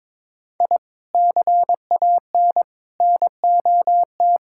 Morse code, heard as a keyed tone, about 22 wpm, 700 Hz. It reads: I CAN NOT
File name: practice / 22wpm / Build A Phrase - 5-Word Phrases - 22wpm